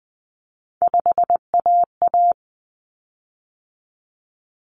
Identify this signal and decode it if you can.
Morse code, heard as a keyed tone, about 20 wpm, 700 Hz. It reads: 5AA
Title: Common Cut Numbers - 3-Digit 20wpm